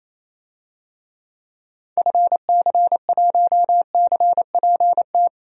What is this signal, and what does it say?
Morse code, heard as a keyed tone, about 28 wpm, 700 Hz. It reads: FC1CPT
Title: DXCC-100 Call Signs - 6 Characters - 28wpm